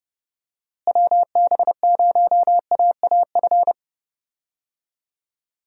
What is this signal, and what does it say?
Morse code, heard as a keyed tone, about 31 wpm, 700 Hz. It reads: WB0AAF
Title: USA Call Signs - 6 Characters - Repeated 3x - 30wpm